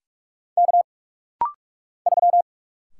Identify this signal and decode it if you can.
Morse code, heard as a keyed tone, about 44 wpm, 700 Hz. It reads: K 3
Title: Single Letter-Number - Rapid-Fire 45wpm